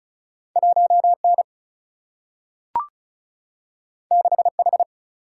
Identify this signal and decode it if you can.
Morse code, heard as a keyed tone, about 36 wpm, 700 Hz. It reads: 1N 6H
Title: Single-Letter Single-Number 35wpm